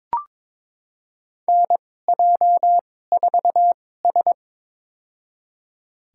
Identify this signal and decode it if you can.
Morse code, heard as a keyed tone, about 22 wpm, spaced as for 16 wpm, 700 Hz. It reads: NJ4S
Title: USA Call Signs - 4 Characters - 2x Character Spacing - 22wpm